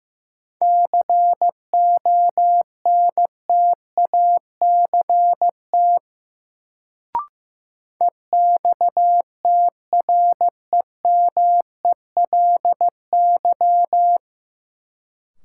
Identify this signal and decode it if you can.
Morse code, heard as a keyed tone, about 15 wpm, 700 Hz. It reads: CONTACT EXTREMELY
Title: Sentences from Top 1000 Words - Review Part 7 - New Word Drill - 15wpm